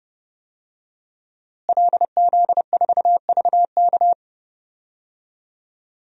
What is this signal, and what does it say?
Morse code, heard as a keyed tone, about 30 wpm, 700 Hz. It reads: LZ4VK
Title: DXCC-100 Call Signs - 5 Characters - Repeated 3x - 30wpm